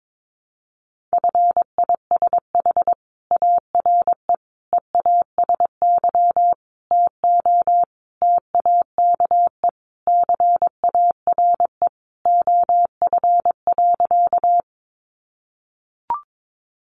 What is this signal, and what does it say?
Morse code, heard as a keyed tone, about 22 wpm, 700 Hz. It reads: FISH ARE EASY TO TAKE CARE OF.
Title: Easy Phrases 22wpm